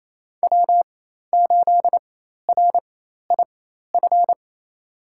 Morse code, heard as a keyed tone, about 28 wpm, spaced as for 13 wpm, 700 Hz. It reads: W8RIF
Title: USA Call Signs - 4x Character Spacing - 28wpm